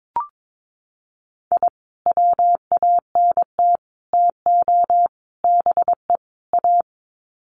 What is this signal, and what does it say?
Morse code, heard as a keyed tone, about 22 wpm, 700 Hz. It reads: I WANT TO BE A
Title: Sets of 5 Words Encore 22wpm